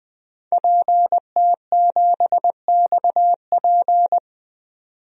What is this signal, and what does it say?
Morse code, heard as a keyed tone, about 20 wpm, 700 Hz. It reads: PT7XP
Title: DXCC-100 Call Signs - 5 Characters - 20wpm